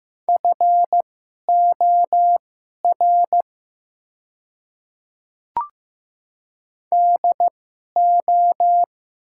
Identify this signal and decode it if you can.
Morse code, heard as a keyed tone, about 16 wpm, spaced as for 11 wpm, 700 Hz. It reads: FOR DO
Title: 2x Character Spacing - Top 200 Words - 15wpm